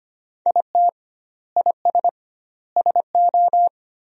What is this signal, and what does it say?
Morse code, heard as a keyed tone, about 25 wpm, 700 Hz. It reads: IT IS SO
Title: 2x Word-Spacing - Sentences from Top 300 Words - 25wpm